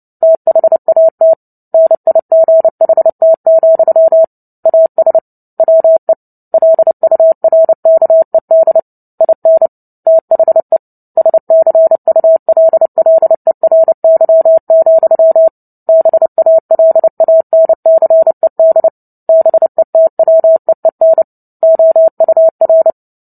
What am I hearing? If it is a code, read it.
Morse code, heard as a keyed tone, about 30 wpm, 650 Hz. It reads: THAT NIGHT, AS WE LURKED IN THE SCULLERY, BALANCED BETWEEN OUR
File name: Book Two / 03-Chapter 3 - 29 WPM